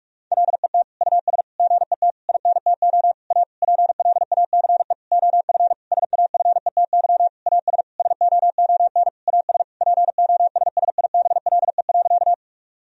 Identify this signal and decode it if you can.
Morse code, heard as a keyed tone, about 46 wpm, 700 Hz. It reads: LET US GET INTO A PLACE OF SAFETY AS SOON AS POSSIBLE.
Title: Sentences from Top 1000 Words - Review Part 4 - 45wpm